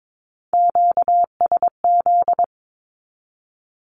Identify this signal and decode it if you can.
Morse code, heard as a keyed tone, about 22 wpm, 700 Hz. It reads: QSZ
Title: Q-Codes Repeated 3x 22wpm